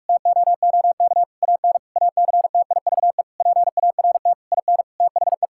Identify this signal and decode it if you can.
Morse code, heard as a keyed tone, about 41 wpm, 700 Hz. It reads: TOOK AN ACTIVE PART IN THE
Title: Sentences from Top 1000 Words - Review Part 6 - 45wpm